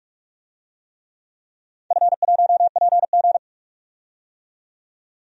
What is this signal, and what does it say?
Morse code, heard as a keyed tone, about 46 wpm, 700 Hz. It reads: F1PG
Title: DXCC-100 Call Signs - 4 Characters - 45wpm